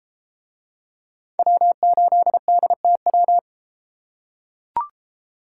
Morse code, heard as a keyed tone, about 29 wpm, 700 Hz. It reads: W8DTW